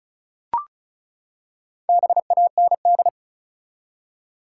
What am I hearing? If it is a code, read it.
Morse code, heard as a keyed tone, about 36 wpm, 700 Hz. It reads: BAND
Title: Four Letter Words - 35wpm